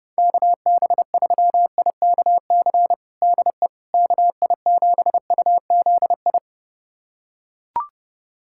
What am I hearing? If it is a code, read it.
Morse code, heard as a keyed tone, about 31 wpm, 700 Hz. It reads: KB3IKC DE KI7UZI
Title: USA Callsign DE Callsign - 6 Characters - 30wpm